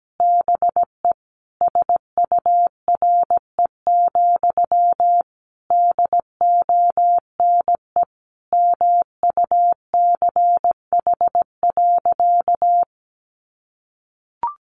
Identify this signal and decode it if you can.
Morse code, heard as a keyed tone, about 17 wpm, 700 Hz. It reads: BE SURE, DONE MUCH.
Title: Sentences from Top 500 Words 17wpm